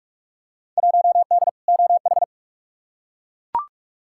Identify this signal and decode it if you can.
Morse code, heard as a keyed tone, about 41 wpm, 700 Hz. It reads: 1D OH
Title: Field Day Exchanges - 45wpm